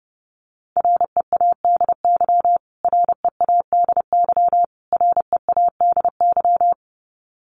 Morse code, heard as a keyed tone, about 30 wpm, 700 Hz. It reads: READY READY READY
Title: Top 500 Words Repeated 30wpm